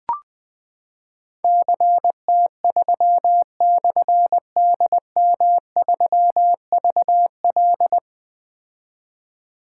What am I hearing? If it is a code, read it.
Morse code, heard as a keyed tone, about 20 wpm, 700 Hz. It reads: CT3/DM3VL
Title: Difficult Call Signs 20wpm